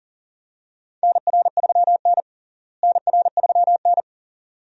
Morse code, heard as a keyed tone, about 40 wpm, 700 Hz. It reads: NR3N NR3N
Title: USA Call Signs - 4 Characters - Repeated 2x - 40wpm